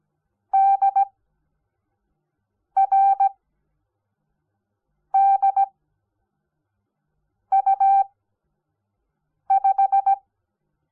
Morse code, heard as a keyed tone, about 17 wpm, spaced as for 5 wpm, 780 Hz. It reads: DRDU5
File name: Onlinelehrgang / morsen / DJ4UF Morsen05